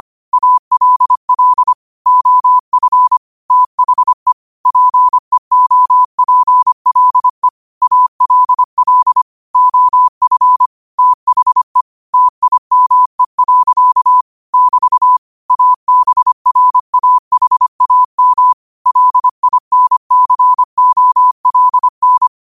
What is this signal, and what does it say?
Morse code, heard as a keyed tone, about 25 wpm, 1000 Hz. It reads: ALL OF THE PEOPLE ALL OF THE TIME. <BT> ABRAHAM LINCOLN
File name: qotdfiles / 25wpm